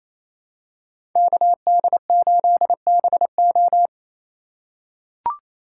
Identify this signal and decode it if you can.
Morse code, heard as a keyed tone, about 27 wpm, 700 Hz. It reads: KD8BO
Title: USA Call Signs - 5 Characters - 4x Character Spacing - 28wpm